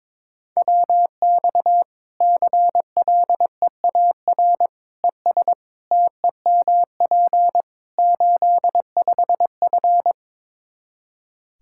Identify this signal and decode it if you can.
Morse code, heard as a keyed tone, about 22 wpm, 700 Hz. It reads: WX CLEAR ES TEMP 85F
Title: Practice QSO - Exchange 2 - Info - 22wpm